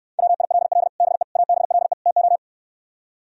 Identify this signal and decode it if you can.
Morse code, heard as a keyed tone, about 66 wpm, 700 Hz. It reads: WILL BE ABLE TO